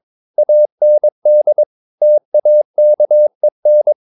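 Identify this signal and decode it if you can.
Morse code, heard as a keyed tone, about 22 wpm, 600 Hz. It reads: AND TAKEN
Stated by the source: Morse code practice at 22 wpm, sending text from an English novel.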